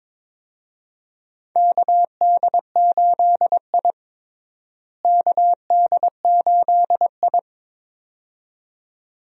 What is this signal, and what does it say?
Morse code, heard as a keyed tone, about 22 wpm, 700 Hz. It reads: KD8I KD8I
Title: USA Call Signs - 4 Characters - Repeated 2x - 22wpm